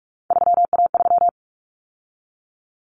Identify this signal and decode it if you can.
Morse code, heard as a keyed tone, about 44 wpm, 700 Hz. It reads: 3A3
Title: Common Cut Numbers - 3-Digit 45wpm